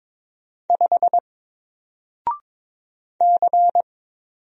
Morse code, heard as a keyed tone, about 22 wpm, 700 Hz. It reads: 5 C
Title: Single Letter-Number Alt 22wpm